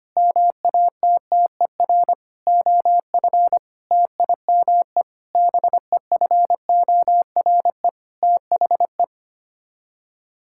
Morse code, heard as a keyed tone, about 25 wpm, 700 Hz. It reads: MATTER OF TIME BEFORE THE
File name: Sets of 9 Words - 25wpm